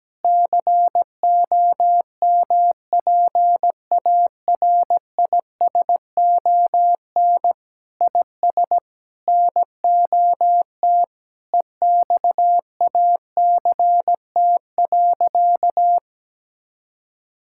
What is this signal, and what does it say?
Morse code, heard as a keyed tone, about 17 wpm, 700 Hz. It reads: COMPARISON IS NOT EXACT.
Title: Sentences from Top 1600 Words 17wpm